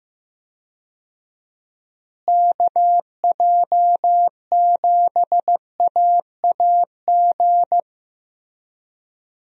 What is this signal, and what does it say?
Morse code, heard as a keyed tone, about 16 wpm, 700 Hz. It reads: KJ7AAG
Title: USA Call Signs - 6 Characters - Repeated 2x - 15wpm